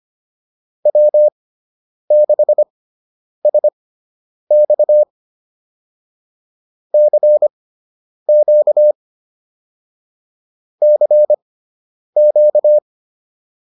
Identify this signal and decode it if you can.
Morse code, heard as a keyed tone, about 25 wpm, spaced as for 9 wpm, 600 Hz. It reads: W6SX CQ CQ